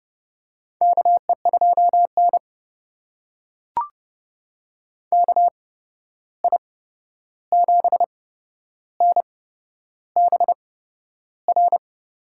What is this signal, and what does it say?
Morse code, heard as a keyed tone, about 29 wpm, spaced as for 11 wpm, 700 Hz. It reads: KE2N KI7NBR
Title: USA Call Signs - 8x Character Spacing - 30wpm